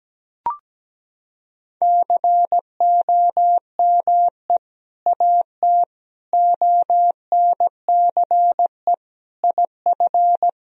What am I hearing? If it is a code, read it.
Morse code, heard as a keyed tone, about 17 wpm, 700 Hz. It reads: COME AT ONCE IF
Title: Sentences from Top 1000 Words - Review Part 2 - 17wpm